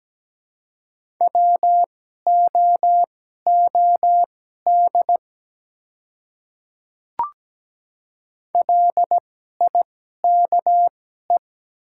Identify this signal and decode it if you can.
Morse code, heard as a keyed tone, about 17 wpm, spaced as for 12 wpm, 700 Hz. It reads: WOOD LIKE
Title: 2x Character Spacing - Top 300 Words - 17wpm